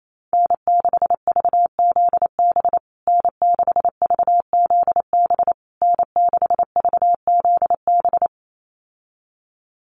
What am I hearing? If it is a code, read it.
Morse code, heard as a keyed tone, about 28 wpm, 700 Hz. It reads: N6VZB N6VZB N6VZB
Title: USA Call Signs Repeated 3x 28wpm